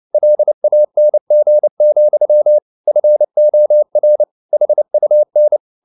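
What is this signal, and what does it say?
Morse code, heard as a keyed tone, about 29 wpm, 600 Hz. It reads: LANG, FOR HUN
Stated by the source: Fast (QRQ) Morse at 29 wpm, from CW med Gnister.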